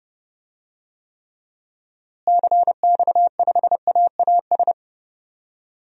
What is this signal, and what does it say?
Morse code, heard as a keyed tone, about 30 wpm, 700 Hz. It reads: CX5AAS
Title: DXCC-100 Call Signs - 6 Characters - Repeated 3x - 30wpm